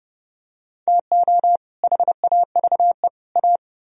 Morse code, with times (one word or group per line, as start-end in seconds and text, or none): 0.9-1.6 TO
1.8-3.1 HAVE
3.4-3.6 A